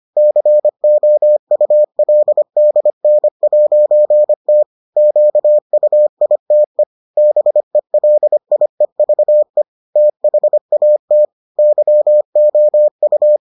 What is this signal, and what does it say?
Morse code, heard as a keyed tone, about 25 wpm, 600 Hz. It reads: COULDN'T QUITE BELIEVE THAT YOU